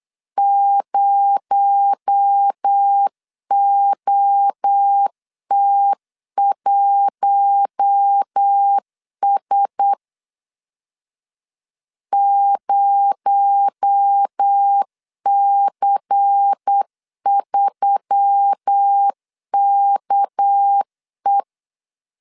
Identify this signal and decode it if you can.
Morse code, heard as a keyed tone, about 8 wpm, 800 Hz. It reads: 0OT1S 0C3KE
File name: LEZIONI / 8 YJ45 / 8 YJ45 8wpm 3214